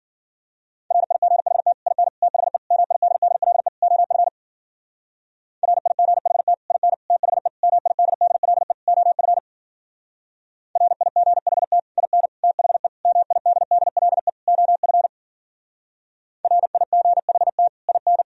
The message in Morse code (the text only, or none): RIGHT IN THE MIDDLE OF RIGHT IN THE MIDDLE OF RIGHT IN THE MIDDLE OF RIGHT IN